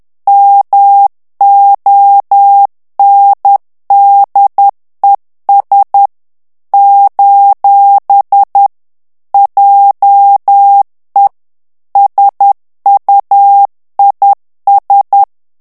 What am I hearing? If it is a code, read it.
Morse code, heard as a keyed tone, about 10 wpm, 800 Hz. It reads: MONDES : JE SUIS
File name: morse mp3 / FRANCAIS TEXTE CLAIR 2 VITESSE 10 MOTS MINUTES